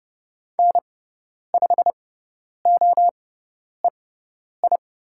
Morse code, heard as a keyed tone, about 30 wpm, spaced as for 10 wpm, 690 Hz. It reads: N5OEI
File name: Lesson 18g - Letter m - Call Signs 30wpm